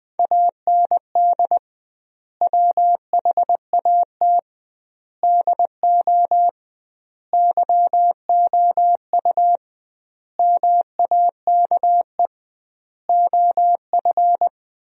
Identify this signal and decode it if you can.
Morse code, heard as a keyed tone, about 20 wpm, 700 Hz. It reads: AND WHAT DO YOU MAKE OF
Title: 2x Word-Spacing - Sentences from Top 100 Words - 20wpm